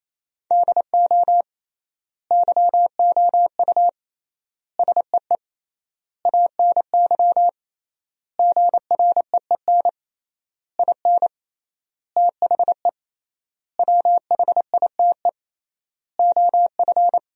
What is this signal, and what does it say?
Morse code, heard as a keyed tone, about 27 wpm, 700 Hz. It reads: DO YOU SEE ANY GREEN IN THE WHITE OF
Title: 3x Word-Spacing - Sentences from Top 500 Words - 28wpm